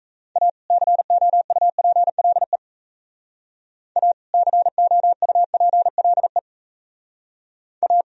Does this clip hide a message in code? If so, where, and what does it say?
Yes: Morse code from 0.4 to 8.0 s, heard as a keyed tone, about 36 wpm, 700 Hz. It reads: A COUPLE A COUPLE A